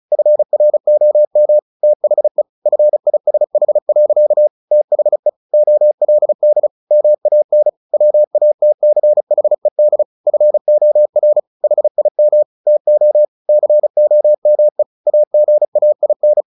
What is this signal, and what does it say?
Morse code, heard as a keyed tone, about 34 wpm, 600 Hz. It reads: FROM THE FISH. THE OLD MAN WATCHED FOR HIM TO COME AGAIN
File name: the old man and the sea 2 35wpm